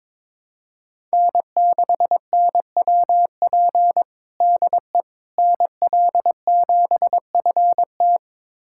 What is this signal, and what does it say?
Morse code, heard as a keyed tone, about 22 wpm, 700 Hz. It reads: N6NWP DE NL7FT
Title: USA Callsign DE Callsign - 5 Characters - 22wpm